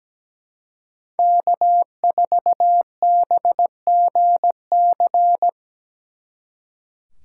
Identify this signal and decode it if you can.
Morse code, heard as a keyed tone, about 20 wpm, 700 Hz. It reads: K4BGC
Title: Speed Racing - USA 1x3 Call Signs 17wpm